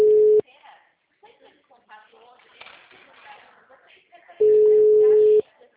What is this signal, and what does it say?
Ringback tone (Europe, 425 Hz), with speech.